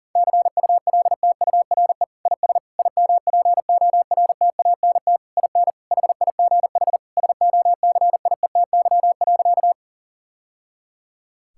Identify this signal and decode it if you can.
Morse code, heard as a keyed tone, about 40 wpm, 700 Hz. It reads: CULTURE IS IMPORTANT IN HIGH SOCIETY.
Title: Sentences from Top 1000 Words - Review Part 6 - 40wpm